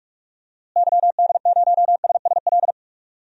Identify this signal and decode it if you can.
Morse code, heard as a keyed tone, about 46 wpm, 700 Hz. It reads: YD0SSL
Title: DXCC-100 Call Signs - 6 Characters - Repeated 2x - 45wpm